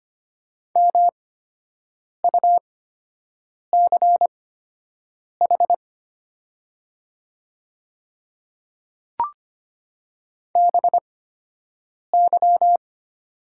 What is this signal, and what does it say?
Morse code, heard as a keyed tone, about 25 wpm, spaced as for 7 wpm, 700 Hz. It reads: MUCH BY
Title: 8x Character Spacing - Top 200 Words - 25wpm